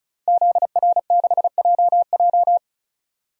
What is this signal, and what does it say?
Morse code, heard as a keyed tone, about 34 wpm, 700 Hz. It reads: ZR6JJ
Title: DXCC-100 Call Signs - 5 Characters - 35wpm